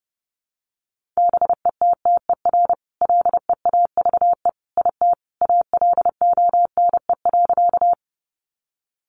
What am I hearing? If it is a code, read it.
Morse code, heard as a keyed tone, about 30 wpm, 700 Hz. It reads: BETTER LEAVE IT ALONE.
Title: Sentences from Top 1000 Words 30wpm